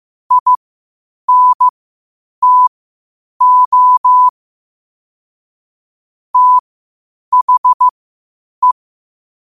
Morse code, heard as a keyed tone, about 15 wpm, spaced as for 8 wpm, 1000 Hz. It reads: INTO THE
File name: qotdfiles / 8wpm